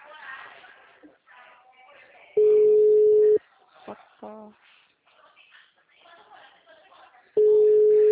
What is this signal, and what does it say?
Ringback tone (Europe, 425 Hz), with speech.
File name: out- -2011- - - .31395